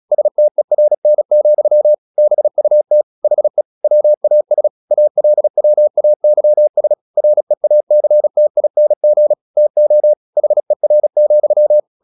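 Morse code, heard as a keyed tone, about 36 wpm, 600 Hz. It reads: STERN, BUT HE WAS ALWAYS REACTING TO HER,
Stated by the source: Morse practice at 36 wpm, text of Ken Follett's novel Never, from CW med Gnister.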